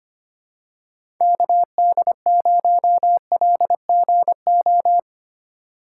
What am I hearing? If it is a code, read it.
Morse code, heard as a keyed tone, about 25 wpm, 700 Hz. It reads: KD0LGO
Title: USA Call Signs - 6 Characters - 3x Character Spacing - 25wpm